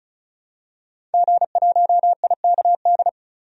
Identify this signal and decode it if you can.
Morse code, heard as a keyed tone, about 36 wpm, 700 Hz. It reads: G1IKD
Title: DXCC-100 Call Signs - 5 Characters - 35wpm